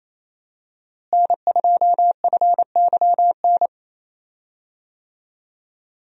Morse code, heard as a keyed tone, about 28 wpm, 700 Hz. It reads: N2FYN
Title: USA Call Signs - 5 Characters - Repeated 2x - 28wpm